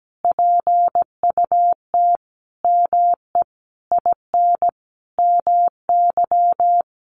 Morse code, heard as a keyed tone, about 17 wpm, 700 Hz. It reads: PUT ME IN MY
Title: Sentences from Top 1200 Words 17wpm